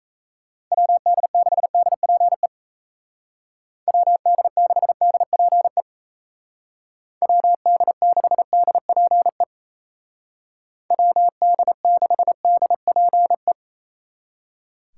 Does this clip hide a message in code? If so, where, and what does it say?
Yes: Morse code from 0.7 to 13.5 s, heard as a keyed tone, about 37 wpm, 700 Hz. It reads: WD6DPE WD6DPE WD6DPE WD6DPE